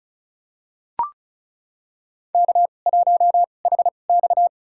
Morse code, heard as a keyed tone, about 34 wpm, spaced as for 25 wpm, 700 Hz. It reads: K1HX
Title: USA Call Signs - 4 Characters - 2x Character Spacing - 35wpm